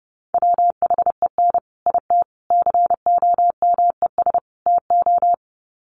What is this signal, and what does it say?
Morse code, heard as a keyed tone, about 30 wpm, 700 Hz. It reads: WHEN IT COMES TO
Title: Sets of 4 Words 30wpm